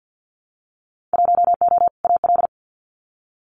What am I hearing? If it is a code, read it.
Morse code, heard as a keyed tone, about 50 wpm, 700 Hz. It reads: 1O AL
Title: Winter Field Day Exchanges 50wpm